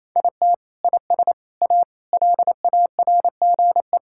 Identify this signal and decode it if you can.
Morse code, heard as a keyed tone, about 27 wpm, 700 Hz. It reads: IT IS A LARGE
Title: Sentences from Top 1000 Words - Review Part 6 - 28wpm